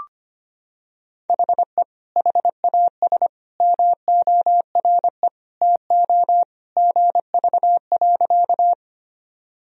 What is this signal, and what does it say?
Morse code, heard as a keyed tone, about 25 wpm, 700 Hz. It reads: HE HAS MORE TO GV.
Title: Sentences from Top 200 Words and Abbreviations 25wpm